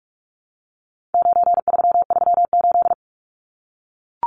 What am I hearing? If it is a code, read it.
Morse code, heard as a keyed tone, about 44 wpm, 700 Hz. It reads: 9338
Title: Numbers - 4-Digit 45wpm